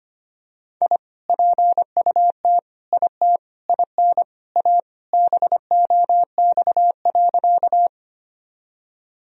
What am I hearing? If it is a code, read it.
Morse code, heard as a keyed tone, about 25 wpm, 700 Hz. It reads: I PUT IT IN A BOX.
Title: Sentences from Top 1000 Words - Review Part 2 - 25wpm